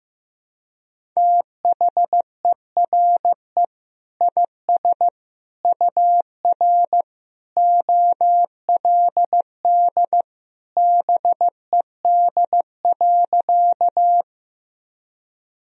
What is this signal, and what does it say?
Morse code, heard as a keyed tone, about 15 wpm, 700 Hz. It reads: THERE IS UR OLD BED.
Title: Sentences from Top 500 Words and Abbreviations 15wpm